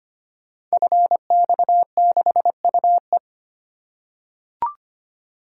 Morse code, heard as a keyed tone, about 25 wpm, 700 Hz. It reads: FX6UE
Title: Code Groups - 5 Letter:Numbers - 25wpm